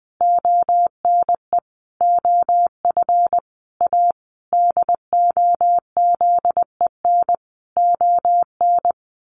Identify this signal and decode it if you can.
Morse code, heard as a keyed tone, about 20 wpm, 700 Hz. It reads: ONE OF A DOZEN ON
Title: A Sound Of Thunder 20wpm